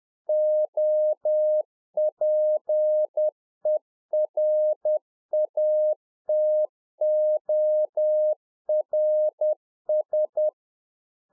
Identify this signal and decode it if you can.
Morse code, heard as a keyed tone, about 10 wpm, 610 Hz. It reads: OPERATORS
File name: UNFOGETTABLE QSO 10WPM